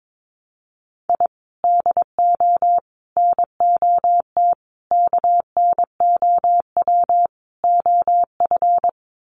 Morse code, heard as a keyed tone, about 22 wpm, 700 Hz. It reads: I DO NOT KNOW OF
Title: Sets of 5 Words Encore2 22wpm